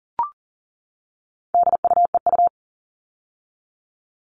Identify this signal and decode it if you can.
Morse code, heard as a keyed tone, about 40 wpm, 700 Hz. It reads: DUEU
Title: All Cut Numbers - 4-Digit 40wpm